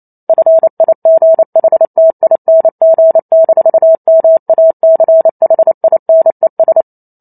Morse code, heard as a keyed tone, about 28 wpm, 650 Hz. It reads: FIGHTING-MACHINES